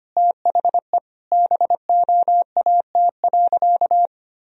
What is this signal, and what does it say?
Morse code, heard as a keyed tone, about 25 wpm, 700 Hz. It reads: THE BOAT.
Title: Sentences from Top 1000 Words - Review Part 4 - 25wpm